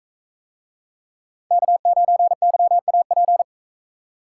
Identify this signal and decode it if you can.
Morse code, heard as a keyed tone, about 44 wpm, 700 Hz. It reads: K9YAP